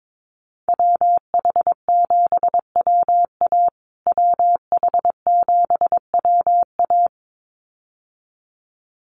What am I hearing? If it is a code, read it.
Morse code, heard as a keyed tone, about 22 wpm, 700 Hz. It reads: WH7WA WH7WA
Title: USA Call Signs Repeated 2x 22wpm